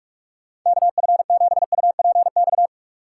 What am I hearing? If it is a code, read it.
Morse code, heard as a keyed tone, about 46 wpm, 700 Hz. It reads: KF7UPX
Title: USA Call Signs - 6 Characters - 8x Character Spacing - 45wpm